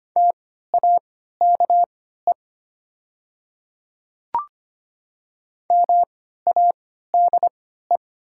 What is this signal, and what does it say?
Morse code, heard as a keyed tone, about 25 wpm, spaced as for 14 wpm, 700 Hz. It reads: TAKE MADE
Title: 3x Character Spacing - Top 200 Words - 25wpm